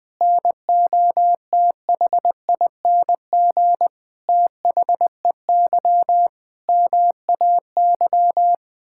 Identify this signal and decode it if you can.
Morse code, heard as a keyed tone, about 20 wpm, 700 Hz. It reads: NOTHING THEY MAY
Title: Sentences from Top 1000 Words - Review Part 6 - 20wpm